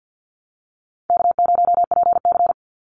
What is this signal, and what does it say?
Morse code, heard as a keyed tone, about 49 wpm, 700 Hz. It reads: K0PC
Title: Easy Call Signs Repeated 2x - 50wpm